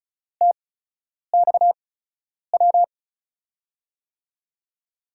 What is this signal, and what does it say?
Morse code, heard as a keyed tone, about 36 wpm, spaced as for 10 wpm, 700 Hz. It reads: TXW
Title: USA Call Signs - 6 Characters - 8x Character Spacing - 35wpm